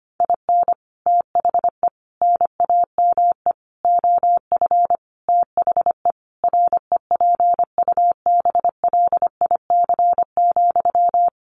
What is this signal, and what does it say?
Morse code, heard as a keyed tone, about 25 wpm, 700 Hz. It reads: IN THE NAME OF THE REPUBLIC,
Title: Sentences from Top 1500 Words 25wpm